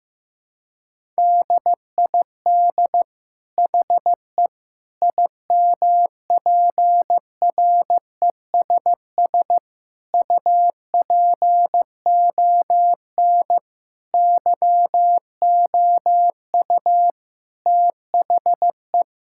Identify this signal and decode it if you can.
Morse code, heard as a keyed tone, about 15 wpm, 700 Hz. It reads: DID HE IMPRESS UPON YOU THE
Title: Sentences from Top 1600 Words 15wpm